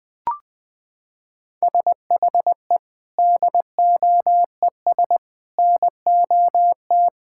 Morse code, heard as a keyed tone, about 21 wpm, 700 Hz. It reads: SHE DOES NOT
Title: Sentences with 4-Word Phrases - 20wpm